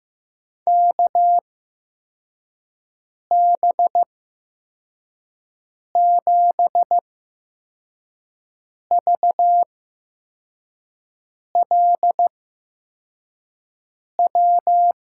Morse code, heard as a keyed tone, about 15 wpm, spaced as for 4 wpm, 700 Hz. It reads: KB7VLW
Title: USA Call Signs - 6 Characters - 8x Character Spacing - 15wpm